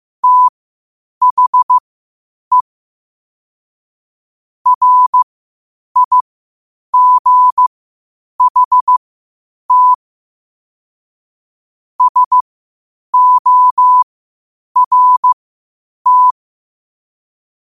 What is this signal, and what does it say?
Morse code, heard as a keyed tone, about 15 wpm, spaced as for 8 wpm, 1000 Hz. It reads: THE RIGHT SORT
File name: qotdfiles / 8wpm